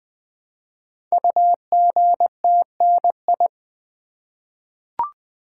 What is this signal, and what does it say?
Morse code, heard as a keyed tone, about 21 wpm, 700 Hz. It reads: UGTNI
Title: Code Groups - 5 Letters - 20wpm